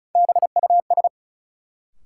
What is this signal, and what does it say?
Morse code, heard as a keyed tone, about 36 wpm, 700 Hz. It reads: BUS
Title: Sentences from Top 1000 Words - Review Part 6 - New Word Drill - 35wpm